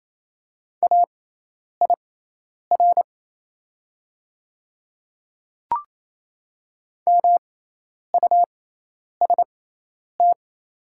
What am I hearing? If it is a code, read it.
Morse code, heard as a keyed tone, about 28 wpm, spaced as for 10 wpm, 700 Hz. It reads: AIR MUST
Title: 6x Character Spacing - Top 200 Words - 28wpm